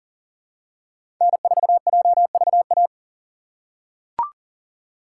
Morse code, heard as a keyed tone, about 40 wpm, 700 Hz. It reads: N4JVA
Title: USA Call Signs - 5 Characters - 3x Character Spacing - 40wpm